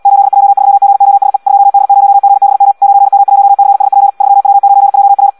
Morse code, heard as a keyed tone, about 75 wpm, 800 Hz. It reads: 8J3GOSE 2D0GBM 9M1CSP 3W9JR
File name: Audio practica CW / qrq-tx-78 2